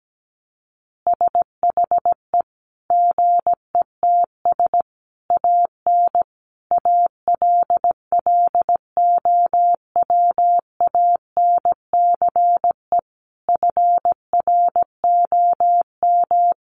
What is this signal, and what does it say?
Morse code, heard as a keyed tone, about 17 wpm, 700 Hz. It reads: SHE GETS AN ALLOWANCE FROM
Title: Easy Phrases 17wpm